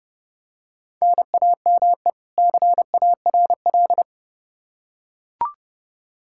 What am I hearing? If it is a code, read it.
Morse code, heard as a keyed tone, about 30 wpm, 700 Hz. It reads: NAME CARL
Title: Practice QSO - Exchange 1 - Beginning - 30wpm